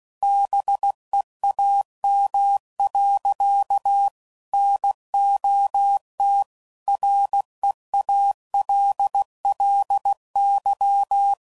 Morse code, heard as a keyed tone, about 16 wpm, 800 Hz. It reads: BEAM. NOT REALLY
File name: CW Code Groups @15WPM / QSOs / WD0BGP Spaced